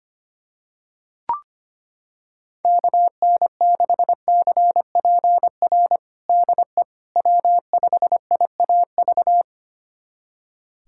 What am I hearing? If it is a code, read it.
Morse code, heard as a keyed tone, about 25 wpm, 700 Hz. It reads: KN6CPR DE W5IAV
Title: USA Callsign DE Callsign 25wpm